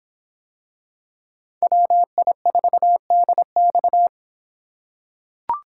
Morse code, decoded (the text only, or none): WI4DX